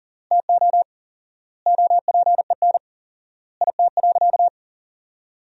Morse code, heard as a keyed tone, about 37 wpm, 700 Hz. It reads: TO OPEN IT.
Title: 4x Word-Spacing - Sentences from Top 300 Words - 40wpm